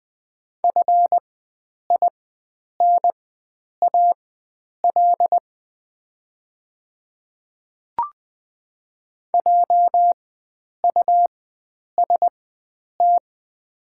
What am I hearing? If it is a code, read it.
Morse code, heard as a keyed tone, about 20 wpm, spaced as for 9 wpm, 700 Hz. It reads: FINAL JUST
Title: 4x Character Spacing - Top 500 Words - 20wpm